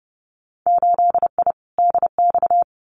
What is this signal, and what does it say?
Morse code, heard as a keyed tone, about 30 wpm, 700 Hz. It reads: 8I DX
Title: Winter Field Day Exchanges Encore 30wpm